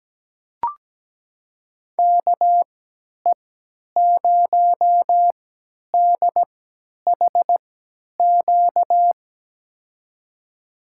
Morse code, heard as a keyed tone, about 17 wpm, spaced as for 10 wpm, 700 Hz. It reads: KE0DHQ
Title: USA Call Signs - 6 Characters - 3x Character Spacing - 17wpm